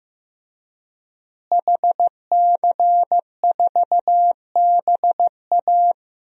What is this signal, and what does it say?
Morse code, heard as a keyed tone, about 15 wpm, 700 Hz. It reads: HC4BA
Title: DXCC-100 Call Signs - 5 Characters - Repeated 3x - 15wpm